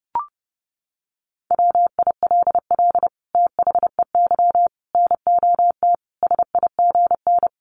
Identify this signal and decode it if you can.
Morse code, heard as a keyed tone, about 30 wpm, 700 Hz. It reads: WILL THEY NOT SIGN
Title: Sentences from Top 1000 Words 30wpm